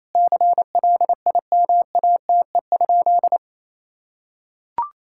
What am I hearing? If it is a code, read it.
Morse code, heard as a keyed tone, about 28 wpm, 700 Hz. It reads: CLIMATE?
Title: Sentences from Top 1600 Words 28wpm